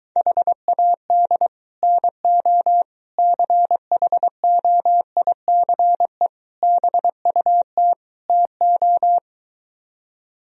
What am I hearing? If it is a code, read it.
Morse code, heard as a keyed tone, about 23 wpm, 700 Hz. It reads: HAD NO CHOICE BUT TO